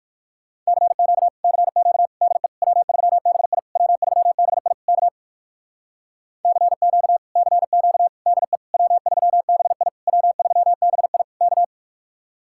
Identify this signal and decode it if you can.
Morse code, heard as a keyed tone, about 59 wpm, 700 Hz. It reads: CQ CQ DE W3BI W3BI K CQ CQ DE W3BI W3BI K